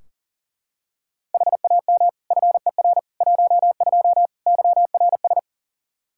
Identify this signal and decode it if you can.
Morse code, heard as a keyed tone, about 40 wpm, 700 Hz. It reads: HAM FER 12 YRS
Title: Practice QSO - Exchange 3 - More Info - 40wpm